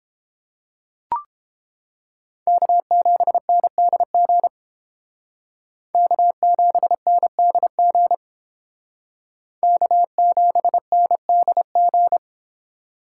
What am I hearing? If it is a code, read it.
Morse code, heard as a keyed tone, about 29 wpm, 700 Hz. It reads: K7NDG K7NDG K7NDG